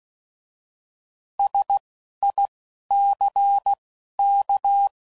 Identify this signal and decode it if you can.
Morse code, heard as a keyed tone, about 16 wpm, 800 Hz. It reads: S I C K